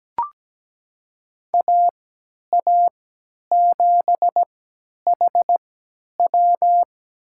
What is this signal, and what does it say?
Morse code, heard as a keyed tone, about 17 wpm, spaced as for 10 wpm, 700 Hz. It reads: AA7HW
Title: USA Call Signs - 5 Characters - 3x Character Spacing - 17wpm